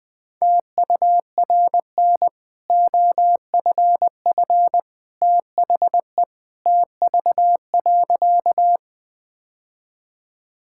Morse code, heard as a keyed tone, about 20 wpm, 700 Hz. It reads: TURN OFF THE TV.
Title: Easy Sentences with Hints - 20wpm